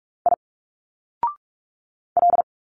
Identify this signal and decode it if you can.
Morse code, heard as a keyed tone, about 46 wpm, 700 Hz. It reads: I L
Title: Single Letters 45wpm